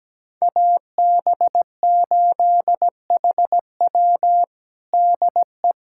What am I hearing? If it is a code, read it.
Morse code, heard as a keyed tone, about 17 wpm, 700 Hz. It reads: AB8HW DE
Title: USA Callsign DE Callsign - 5 Characters - 17wpm